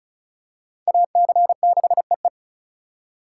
Morse code, heard as a keyed tone, about 36 wpm, 700 Hz. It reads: AC6EE
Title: USA Call Signs - 5 Characters - 2x Character Spacing - 35wpm